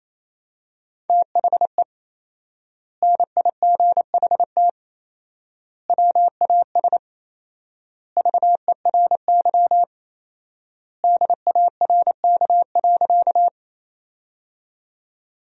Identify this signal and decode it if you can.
Morse code, heard as a keyed tone, about 27 wpm, 700 Hz. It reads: THE NIGHT WAS VERY DARK.
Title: 4x Word-Spacing - Sentences from Top 500 Words - 28wpm